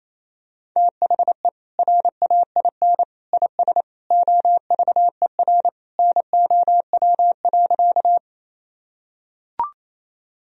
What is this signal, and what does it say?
Morse code, heard as a keyed tone, about 28 wpm, 700 Hz. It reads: THE RAIN IS OVER NOW.
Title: Sentences from Top 1000 Words - Review Part 1 - 28wpm